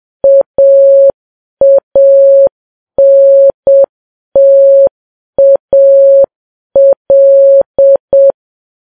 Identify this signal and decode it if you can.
Morse code, heard as a keyed tone, about 7 wpm, 550 Hz. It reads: AANTAL
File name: Files Web tls / Morse on9fv / morse7kps